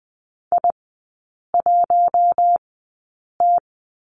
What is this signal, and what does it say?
Morse code, heard as a keyed tone, about 20 wpm, spaced as for 8 wpm, 700 Hz. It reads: I1T
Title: 041 Lesson 8e - Number 1 - Three Characters 20wpm